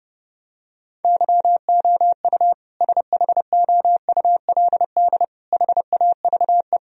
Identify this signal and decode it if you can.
Morse code, heard as a keyed tone, about 32 wpm, 700 Hz. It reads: YOU SHOULD HAVE
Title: Sentences from Top 1000 Words - Review Part 1 - 30wpm